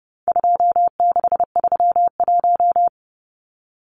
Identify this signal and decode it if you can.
Morse code, heard as a keyed tone, about 30 wpm, 700 Hz. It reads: 2631
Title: Numbers - 4-Digit 30wpm